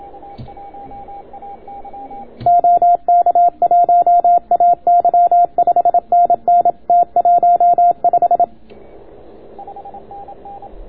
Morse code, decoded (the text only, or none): OK1AY 5NN T15